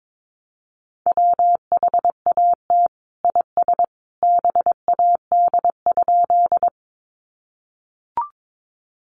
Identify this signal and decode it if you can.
Morse code, heard as a keyed tone, about 22 wpm, 700 Hz. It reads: WHAT IS BAD?
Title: Sentences from Top 700 Words 22wpm